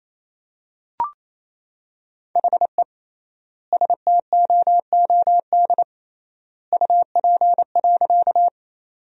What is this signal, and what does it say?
Morse code, heard as a keyed tone, about 28 wpm, 700 Hz. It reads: HE STOOD UP.
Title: 3x Word-Spacing - Sentences from Top 500 Words - 28wpm